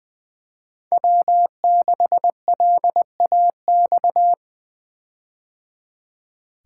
Morse code, heard as a keyed tone, about 20 wpm, 700 Hz. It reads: W6LAX
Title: K1USN SST QSO Practice - 20wpm